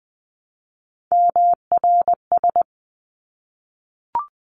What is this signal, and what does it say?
Morse code, heard as a keyed tone, about 20 wpm, 700 Hz. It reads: MRS
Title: 260 Lesson 42e - Words 20-20wpm